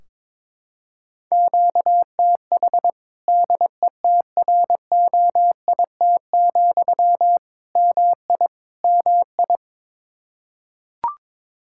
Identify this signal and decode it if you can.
Morse code, heard as a keyed tone, about 21 wpm, 700 Hz. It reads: QTH DETROIT, MI MI
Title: Practice QSO - Exchange 1 - Beginning - 22wpm